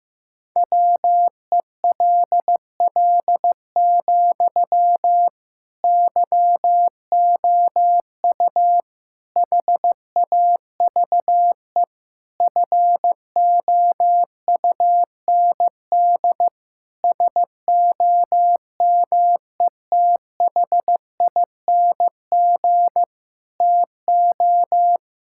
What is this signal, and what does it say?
Morse code, heard as a keyed tone, about 15 wpm, 700 Hz. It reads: WELL, YOU HAVE FOUND SOMETHING TO
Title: Sentences from Top 1000 Words - Review Part 6 - 15wpm